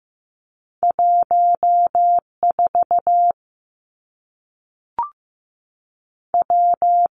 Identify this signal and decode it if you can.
Morse code, heard as a keyed tone, about 15 wpm, 700 Hz. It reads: 1 4 W
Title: Two Challenging Letter-Number Combinations 15wpm